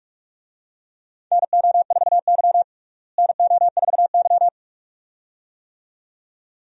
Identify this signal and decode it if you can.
Morse code, heard as a keyed tone, about 41 wpm, 700 Hz. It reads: NO4Y NO4Y
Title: USA Call Signs - 4 Characters - Repeated 2x - 45wpm